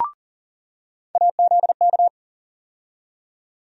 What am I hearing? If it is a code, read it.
Morse code, heard as a keyed tone, about 40 wpm, 700 Hz. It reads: AZK
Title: USA Call Sign Postfixes - 3 Letters 40wpm